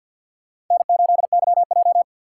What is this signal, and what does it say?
Morse code, heard as a keyed tone, about 54 wpm, 690 Hz. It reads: N8YJ
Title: USA Call Signs - 4 Characters - 6x Character Spacing - 50wpm